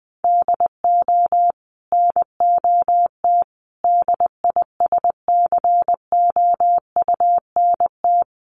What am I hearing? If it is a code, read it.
Morse code, heard as a keyed tone, about 19 wpm, 700 Hz. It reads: DO NOT DISCOUNT
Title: Sentences from Top 1200 Words 20wpm